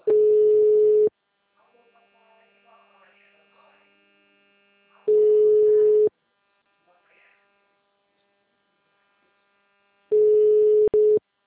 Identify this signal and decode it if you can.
Ringback tone (Europe, 425 Hz), with speech.